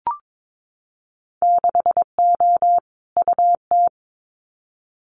Morse code, heard as a keyed tone, about 22 wpm, 700 Hz. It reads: 6O UT
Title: Winter Field Day Exchanges Encore 22wpm